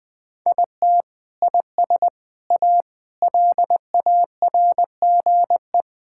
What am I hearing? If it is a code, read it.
Morse code, heard as a keyed tone, about 21 wpm, 700 Hz. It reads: IT IS A LARGE
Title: Sentences from Top 1000 Words - Review Part 5 - 20wpm